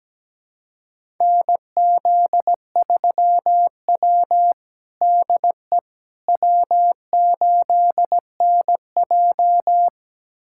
Morse code, heard as a keyed tone, about 17 wpm, 700 Hz. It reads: NZ3W DE W8NJ
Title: USA Callsign DE Callsign - 4 Characters - 17wpm